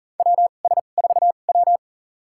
Morse code, heard as a keyed tone, about 37 wpm, spaced as for 28 wpm, 700 Hz. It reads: WS4W
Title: USA Call Signs - 4 Characters - 2x Character Spacing - 40wpm